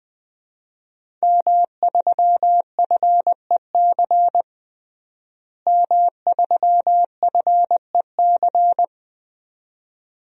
Morse code, heard as a keyed tone, about 20 wpm, 700 Hz. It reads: M3FEC M3FEC
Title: DXCC-100 Call Signs - 5 Characters - Repeated 2x - 20wpm